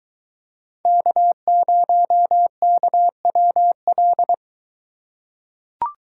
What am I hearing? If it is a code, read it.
Morse code, heard as a keyed tone, about 23 wpm, 700 Hz. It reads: K0KWL